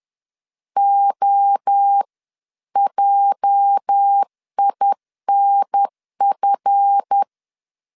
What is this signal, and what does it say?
Morse code, heard as a keyed tone, about 10 wpm, 800 Hz. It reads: OJINF
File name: CW / DET / 1 9 1500C S A 10wpm 4511